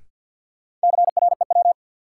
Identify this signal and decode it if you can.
Morse code, heard as a keyed tone, about 53 wpm, 700 Hz. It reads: CREW
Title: Sentences from Top 1500 Words - New Word Drill - 50wpm